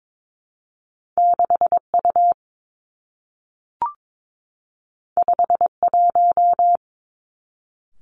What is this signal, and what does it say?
Morse code, heard as a keyed tone, about 22 wpm, 700 Hz. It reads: 6 U 5 1
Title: Two Challenging Letter-Number Combinations 22wpm